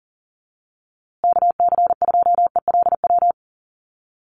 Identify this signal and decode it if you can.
Morse code, heard as a keyed tone, about 40 wpm, 700 Hz. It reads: KC2ELW
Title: USA Call Signs Repeated 2x 40wpm